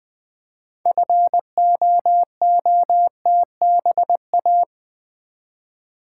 Morse code, heard as a keyed tone, about 20 wpm, 700 Hz. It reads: FOOTBA
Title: Build A Word - 2 Letters At A Time - 8-Letter Words - 20wpm